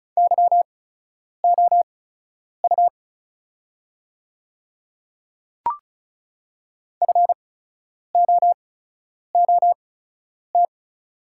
Morse code, heard as a keyed tone, about 34 wpm, spaced as for 10 wpm, 700 Hz. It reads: YOU FOOT
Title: 8x Character Spacing - Top 500 Words - 35wpm